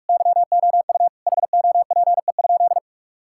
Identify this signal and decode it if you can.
Morse code, heard as a keyed tone, about 46 wpm, 700 Hz. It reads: YOU HOPE?
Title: Sentences from Top 1000 Words - Review Part 5 - 45wpm